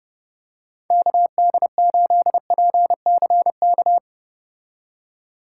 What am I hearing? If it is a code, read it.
Morse code, heard as a keyed tone, about 30 wpm, 700 Hz. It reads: KD8PCK